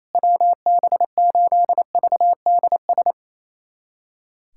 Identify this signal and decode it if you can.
Morse code, heard as a keyed tone, about 28 wpm, 700 Hz. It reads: WB8VDS
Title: USA Callsign DE Callsign - 6 Characters - 28wpm